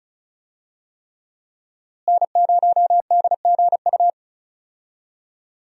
Morse code, heard as a keyed tone, about 34 wpm, 700 Hz. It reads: N0DGU
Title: USA Call Signs - 5 Characters - Repeated 3x - 35wpm